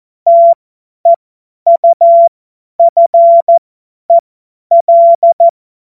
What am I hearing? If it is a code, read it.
Morse code, heard as a keyed tone, about 14 wpm, spaced as for 10 wpm, 680 Hz. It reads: TEUFEL
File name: Morsekurs / Witze / Witz 27 Februar 2024 50 F70